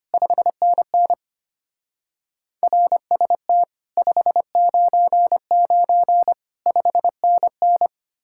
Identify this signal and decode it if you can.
Morse code, heard as a keyed tone, about 32 wpm, 700 Hz. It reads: 5NN RST 599 5NN